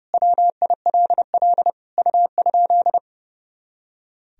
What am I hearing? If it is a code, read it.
Morse code, heard as a keyed tone, about 30 wpm, 700 Hz. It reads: WILL U?
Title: Sentences from Top 400 Words and Abbreviations 30wpm